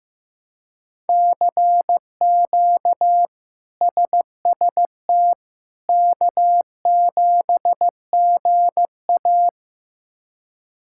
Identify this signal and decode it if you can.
Morse code, heard as a keyed tone, about 16 wpm, 700 Hz. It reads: CQ SST K7GA
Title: K1USN SST QSO Practice - 15wpm